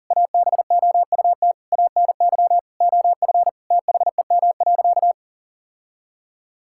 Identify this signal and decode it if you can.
Morse code, heard as a keyed tone, about 40 wpm, 700 Hz. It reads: ABOUT ANY OF THEM.
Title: Sentences from Top 1000 Words - Review Part 6 - 40wpm